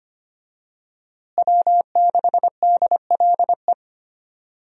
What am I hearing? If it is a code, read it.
Morse code, heard as a keyed tone, about 25 wpm, 700 Hz. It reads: W6DLE
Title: USA Call Signs - 5 Characters - Repeated 3x - 25wpm